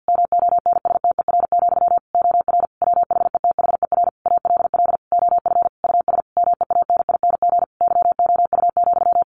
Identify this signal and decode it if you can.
Morse code, heard as a keyed tone, about 50 wpm, 700 Hz. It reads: MONSTER, OR WHETHER ALL OF US MEANING YOU,
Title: A Sound Of Thunder 50wpm